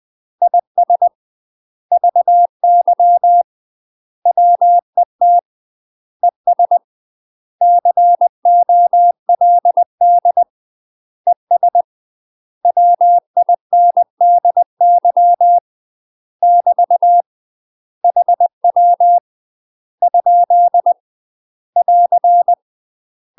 Morse code, heard as a keyed tone, about 20 wpm, 700 Hz. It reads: IS VY WET ES COLD ES WINDY <BT> HW ? <AR>
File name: UK MASTER QSOs TRAINING / 20WPM / UK MASTER QSOs 0070